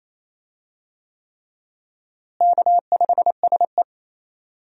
Morse code, heard as a keyed tone, about 28 wpm, 700 Hz. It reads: K5SE
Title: USA Call Signs - 4 Characters - Repeated 3x - 28wpm